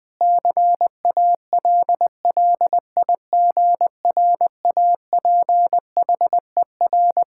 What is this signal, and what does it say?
Morse code, heard as a keyed tone, about 20 wpm, 700 Hz. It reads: CALLIGRAPHER
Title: Pangram Challenge v2 - 20wpm